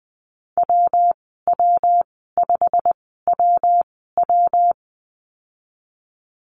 Morse code, heard as a keyed tone, about 20 wpm, spaced as for 15 wpm, 700 Hz. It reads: WW5WW
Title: 128 Lesson 21g - Number 6 - Call Signs 20wpm